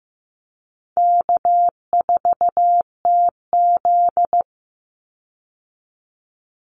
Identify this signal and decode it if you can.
Morse code, heard as a keyed tone, about 15 wpm, 700 Hz. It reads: K4TZ
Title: Call Signs Repeated 3x - 15wpm